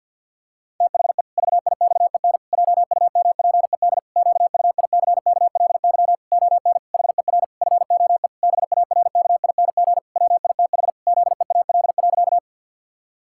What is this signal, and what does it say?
Morse code, heard as a keyed tone, about 50 wpm, 700 Hz. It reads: THE VIXEN JUMPED QUICKLY ON HER FOE BARKING WITH ZEAL.
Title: Pangram Challenge v2 - 50wpm